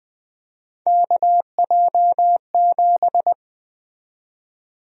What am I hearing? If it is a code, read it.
Morse code, heard as a keyed tone, about 20 wpm, 700 Hz. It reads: KJ7
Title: USA Call Sign Prefixes - 20wpm